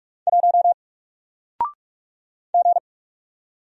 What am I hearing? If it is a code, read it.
Morse code, heard as a keyed tone, about 44 wpm, 700 Hz. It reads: 1 G
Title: Single Letter-Number 45wpm